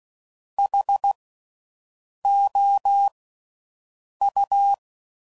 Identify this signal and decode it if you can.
Morse code, heard as a keyed tone, about 16 wpm, spaced as for 6 wpm, 800 Hz. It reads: HOU
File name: Slow With Faster Ings